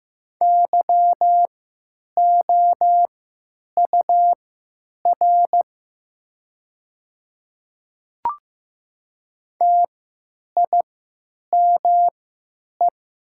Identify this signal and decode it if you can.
Morse code, heard as a keyed tone, about 15 wpm, spaced as for 9 wpm, 700 Hz. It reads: YOUR TIME
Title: 3x Character Spacing - Top 300 Words - 15wpm